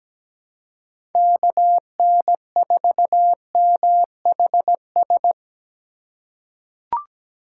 Morse code, heard as a keyed tone, about 17 wpm, 700 Hz. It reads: KN4MHS
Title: USA Call Signs - 6 Characters - 3x Character Spacing - 17wpm